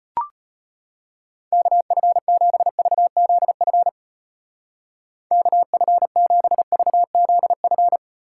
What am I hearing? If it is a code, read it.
Morse code, heard as a keyed tone, about 33 wpm, 700 Hz. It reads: KF7VZF KF7VZF